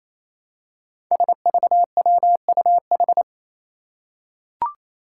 Morse code, heard as a keyed tone, about 27 wpm, 700 Hz. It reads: SVWUH
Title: Five Challenging Letter-Number Combinations 28wpm